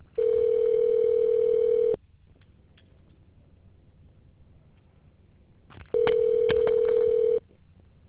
Ringback tone (North America), 440 + 480 Hz.